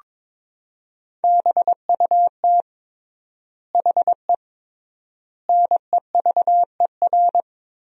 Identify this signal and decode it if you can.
Morse code, heard as a keyed tone, about 22 wpm, 700 Hz. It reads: BUT HE NEVER
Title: 3x Word-Spacing - Sentences from Top 400 Words - 22wpm